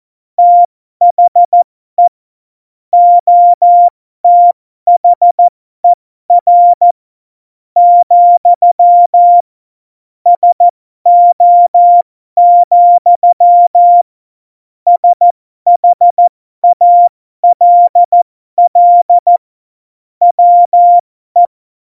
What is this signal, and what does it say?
Morse code, heard as a keyed tone, about 14 wpm, spaced as for 12 wpm, 700 Hz. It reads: THE OTHER, SO, SHALL WE
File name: Morsekurs / Witze